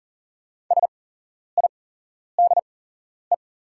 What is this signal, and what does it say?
Morse code, heard as a keyed tone, about 40 wpm, spaced as for 11 wpm, 700 Hz. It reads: SIDE
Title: 8x Character Spacing - Top 300 Words - 40wpm